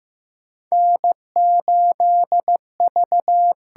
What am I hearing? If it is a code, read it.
Morse code, heard as a keyed tone, about 15 wpm, 700 Hz. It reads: N8V
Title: USA Call Signs - 5 Characters - 8x Character Spacing - 15wpm